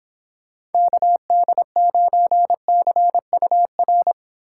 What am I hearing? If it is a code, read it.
Morse code, heard as a keyed tone, about 29 wpm, 700 Hz. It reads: KD9CUR